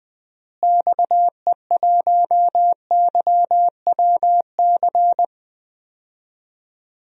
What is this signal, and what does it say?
Morse code, heard as a keyed tone, about 20 wpm, 700 Hz. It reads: XE1YWC
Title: DXCC-100 Call Signs - 6 Characters - Repeated 3x - 20wpm